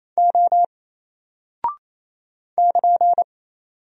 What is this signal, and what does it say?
Morse code, heard as a keyed tone, about 28 wpm, 700 Hz. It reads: O <KN>
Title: Single Character (Letter:Number:Punctuation:Prosign) Alt 28wpm